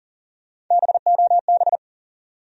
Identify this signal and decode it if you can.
Morse code, heard as a keyed tone, about 40 wpm, 700 Hz. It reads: BOB
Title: USA Common Names - Max 5 Letters - 40wpm